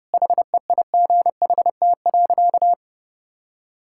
Morse code, heard as a keyed tone, about 30 wpm, 700 Hz. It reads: HEIGHT.
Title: Sentences from Top 1600 Words 30wpm